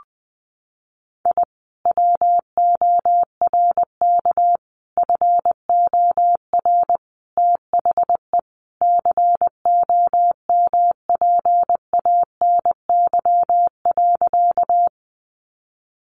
Morse code, heard as a keyed tone, about 20 wpm, 700 Hz. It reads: I WORK FOR THE COMPANY.
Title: Sentences from Top 600 Words 20wpm